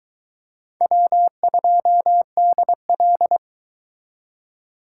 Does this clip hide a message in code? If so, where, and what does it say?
Yes: Morse code from 0.8 to 3.4 s, heard as a keyed tone, about 22 wpm, 700 Hz. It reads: W2DL